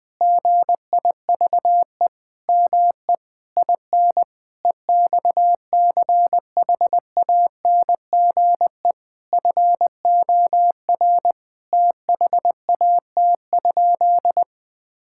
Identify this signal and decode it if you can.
Morse code, heard as a keyed tone, about 20 wpm, 700 Hz. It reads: GIVE ME IN EXCHANGE FOR THAT?
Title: Sentences from Top 1000 Words - Review Part 6 - 20wpm